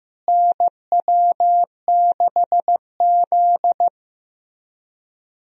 Morse code, heard as a keyed tone, about 15 wpm, 700 Hz. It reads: NW6Z
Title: USA Call Signs - 4 Characters - Repeated 3x - 15wpm